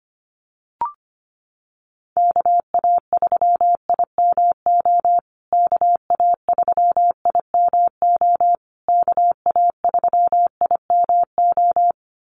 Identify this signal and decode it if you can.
Morse code, heard as a keyed tone, about 25 wpm, 700 Hz. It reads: KA3IMO KA3IMO KA3IMO
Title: USA Call Signs Repeated 3x 25wpm